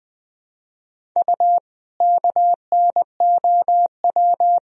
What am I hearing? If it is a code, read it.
Morse code, heard as a keyed tone, about 20 wpm, 700 Hz. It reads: U KNOW
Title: Sentences from Top 500 Words and Abbreviations 20wpm